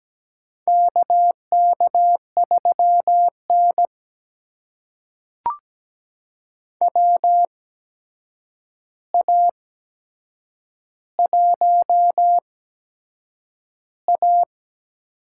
Morse code, heard as a keyed tone, about 17 wpm, spaced as for 7 wpm, 700 Hz. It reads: KK3N WA1A
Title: USA Call Signs - 4 Characters - 8x Character Spacing - 17wpm